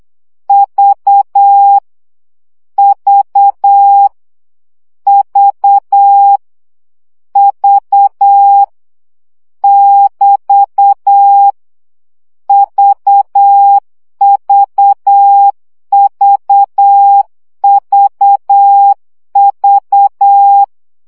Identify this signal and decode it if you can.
Morse code, heard as a keyed tone, about 8 wpm, 800 Hz. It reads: V V V V <BT> VVVVV
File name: CW gyakorlas v 7 baud